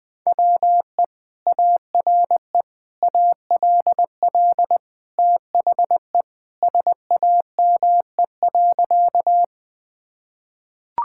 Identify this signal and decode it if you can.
Morse code, heard as a keyed tone, about 21 wpm, 700 Hz. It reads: WE ARE ALL THE SAME.
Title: Sentences with 2-Word Phrases - Spoken Beginning - 20wpm